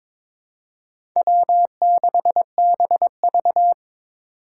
Morse code, heard as a keyed tone, about 22 wpm, 700 Hz. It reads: W6BV
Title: Four Challenging Letter-Number Combinations 22wpm